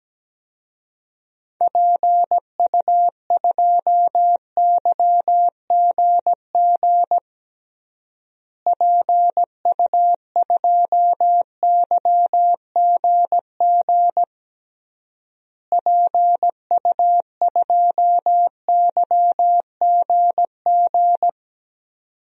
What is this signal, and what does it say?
Morse code, heard as a keyed tone, about 17 wpm, 700 Hz. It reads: PU2YGG PU2YGG PU2YGG
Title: DXCC-100 Call Signs - 6 Characters - Repeated 3x - 17wpm